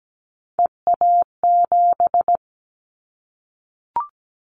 Morse code, heard as a keyed tone, about 17 wpm, 700 Hz. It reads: EA7
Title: Callsign Prefixes for Top 25 DXCC - Repeated 3x - 17wpm